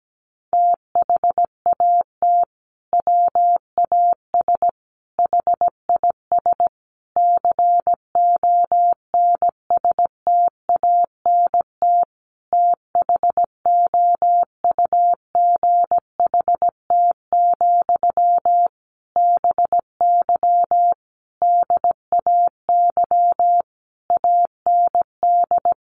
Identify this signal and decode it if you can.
Morse code, heard as a keyed tone, about 17 wpm, 700 Hz. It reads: THAT WAS HIS CONSTANT THOUGHT, BY DAY AND
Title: Sentences from Top 1500 Words 17wpm